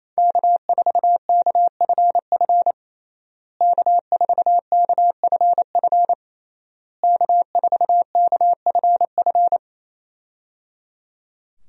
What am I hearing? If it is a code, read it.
Morse code, heard as a keyed tone, about 28 wpm, 700 Hz. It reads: K4KFF K4KFF K4KFF
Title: USA Call Signs - 5 Characters - Repeated 3x - 28wpm